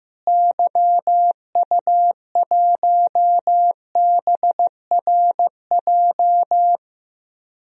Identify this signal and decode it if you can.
Morse code, heard as a keyed tone, about 15 wpm, 700 Hz. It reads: YU1BRJ
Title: DXCC-100 Call Signs - 6 Characters - Repeated 2x - 15wpm